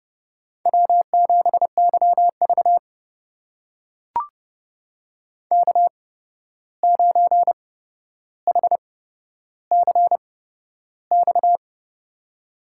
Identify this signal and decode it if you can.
Morse code, heard as a keyed tone, about 30 wpm, spaced as for 12 wpm, 700 Hz. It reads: W7YV K9HCX
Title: USA Call Signs - 8x Character Spacing - 30wpm